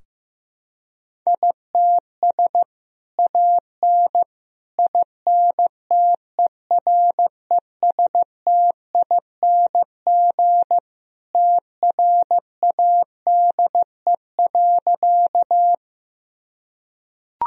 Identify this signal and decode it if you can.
Morse code, heard as a keyed tone, about 16 wpm, 700 Hz. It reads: ITS AN INTERESTING TRADE.
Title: Sentences from Top 1000 Words - Review Part 4 - 15wpm